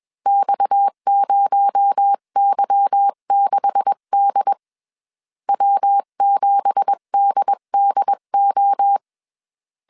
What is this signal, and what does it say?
Morse code, heard as a keyed tone, about 21 wpm, 800 Hz. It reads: X0Y6D W7DDO
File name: CW / DET / 1 9 1500C S B 20wpm 2313